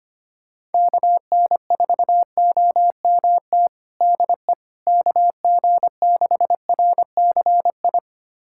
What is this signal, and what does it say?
Morse code, heard as a keyed tone, about 25 wpm, 700 Hz. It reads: KN4OMT DE KG6RCI
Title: USA Callsign DE Callsign - 6 Characters - 25wpm